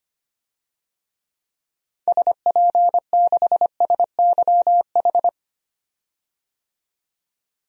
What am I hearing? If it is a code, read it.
Morse code, heard as a keyed tone, about 25 wpm, 700 Hz. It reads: SP6SYH
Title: DXCC-100 Call Signs - 6 Characters - 25wpm